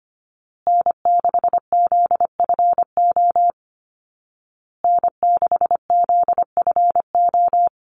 Morse code, heard as a keyed tone, about 25 wpm, 700 Hz. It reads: N6ZFO N6ZFO
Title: Call Signs Repeated 2x - 25wpm